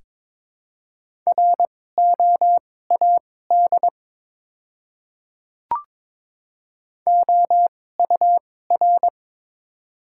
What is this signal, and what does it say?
Morse code, heard as a keyed tone, about 21 wpm, spaced as for 16 wpm, 700 Hz. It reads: ROAD OUR
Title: 2x Character Spacing - Top 500 Words - 22wpm